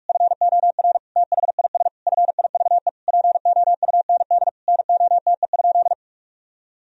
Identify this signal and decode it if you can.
Morse code, heard as a keyed tone, about 41 wpm, 700 Hz. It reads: FOR THIS FIVE POUND NOTE?
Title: Sentences from Top 1000 Words - Review Part 2 - 45wpm